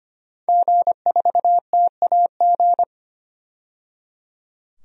Morse code, heard as a keyed tone, about 25 wpm, 700 Hz. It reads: G4TAG
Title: DXCC-100 Call Signs - 5 Characters - 25wpm